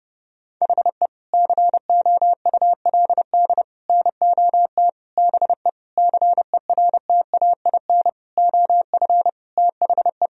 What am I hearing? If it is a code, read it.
Morse code, heard as a keyed tone, about 30 wpm, 700 Hz. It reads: HE COULD NOT BE CERTAIN OF THE
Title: Sentences from Top 1000 Words - Review Part 2 - 30wpm